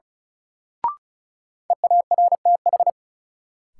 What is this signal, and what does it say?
Morse code, heard as a keyed tone, about 36 wpm, 700 Hz. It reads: EARTH
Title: Sentences from Top 200 Words - New Word Drill - 35wpm